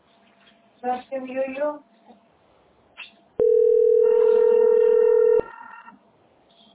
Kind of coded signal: telephone call progress tone